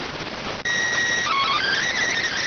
SSTV (slow-scan television), PD 290.